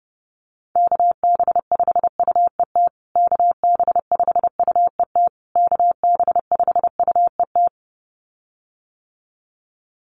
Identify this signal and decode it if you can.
Morse code, heard as a keyed tone, about 30 wpm, 700 Hz. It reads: KB5UET KB5UET KB5UET
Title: USA Call Signs Repeated 3x 30wpm